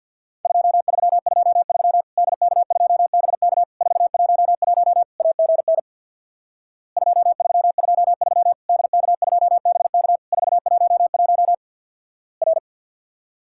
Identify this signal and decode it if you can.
Morse code, heard as a keyed tone, about 50 wpm, 700 Hz. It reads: 2323 DK2BK 411 AGN 2323 DK2BK 411 R
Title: Worked All Europe DX Contest - QTC Exchange with Repeats - 50wpm